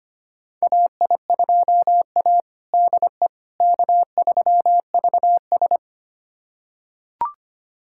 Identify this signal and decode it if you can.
Morse code, heard as a keyed tone, about 25 wpm, 700 Hz. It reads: AI2A DE K3VS
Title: USA Callsign DE Callsign - 4 Characters - 25wpm